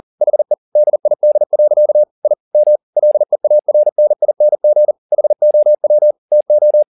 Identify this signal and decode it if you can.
Morse code, heard as a keyed tone, about 40 wpm, 600 Hz. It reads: HE DID. I M LEARNING HOW TO
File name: the old man and the sea 2 40wpm